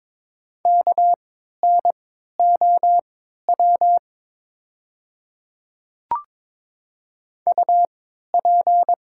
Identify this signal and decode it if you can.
Morse code, heard as a keyed tone, about 22 wpm, spaced as for 12 wpm, 690 Hz. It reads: KNOW UP
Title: 3x Character Spacing - Top 200 Words - 22wpm